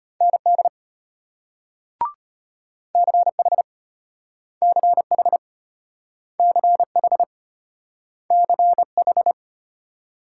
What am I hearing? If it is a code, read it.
Morse code, heard as a keyed tone, about 34 wpm, 700 Hz. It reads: ND CH CH CH CH